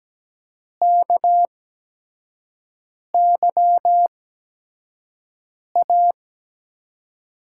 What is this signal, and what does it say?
Morse code, heard as a keyed tone, about 17 wpm, spaced as for 5 wpm, 700 Hz. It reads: KYA
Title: USA Call Signs - 5 Characters - 8x Character Spacing - 17wpm